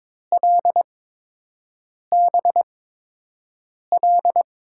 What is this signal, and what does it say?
Morse code, heard as a keyed tone, about 21 wpm, spaced as for 6 wpm, 700 Hz. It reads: LBL
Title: USA Call Signs - 8x Character Spacing - 22wpm